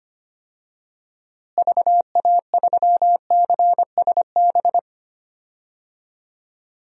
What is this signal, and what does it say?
Morse code, heard as a keyed tone, about 25 wpm, 700 Hz. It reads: VA3CSB
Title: DXCC-100 Call Signs - 6 Characters - Repeated 3x - 25wpm